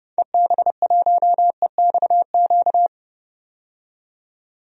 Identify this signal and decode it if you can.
Morse code, heard as a keyed tone, about 30 wpm, 700 Hz. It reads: EB1EXQ
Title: DXCC-100 Call Signs - 6 Characters - 30wpm